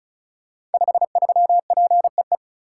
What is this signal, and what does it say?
Morse code, heard as a keyed tone, about 36 wpm, 700 Hz. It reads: 53PEE
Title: Code Groups - 5 Letter:Numbers - 35wpm